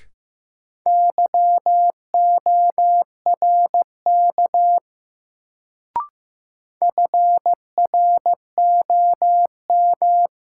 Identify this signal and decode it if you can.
Morse code, heard as a keyed tone, about 15 wpm, 700 Hz. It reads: YORK FROM
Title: Sets of 2 Words - New Word Drill - 15wpm